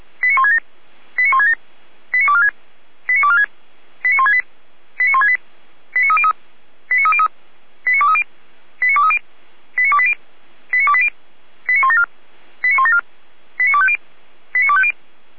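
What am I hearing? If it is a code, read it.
Five-tone selective call (ZVEI), heard as quick runs of five tones, tones of about 70 ms. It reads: 89167 89167 89246 89246 89168 89168 89303 89303 89230 89230 89280 89280 89174 89174 89250 89250
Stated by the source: ZVEI five-tone call received with an SDR, from audio.f85360.feuerwehr-orlamuende.org.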